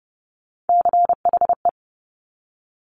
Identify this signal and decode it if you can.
Morse code, heard as a keyed tone, about 30 wpm, 690 Hz. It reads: CHE
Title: Top Three Letter Combinations in Words 30wpm